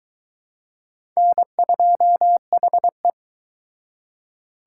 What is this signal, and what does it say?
Morse code, heard as a keyed tone, about 22 wpm, 700 Hz. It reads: N2HE